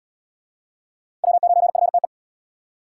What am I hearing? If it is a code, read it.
Morse code, heard as a keyed tone, about 76 wpm, 700 Hz. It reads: W9FTE